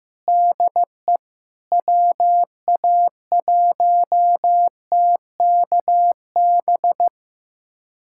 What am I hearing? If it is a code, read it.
Morse code, heard as a keyed tone, about 15 wpm, 700 Hz. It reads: DE WA1TKB
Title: USA Callsign DE Callsign - 6 Characters - 15wpm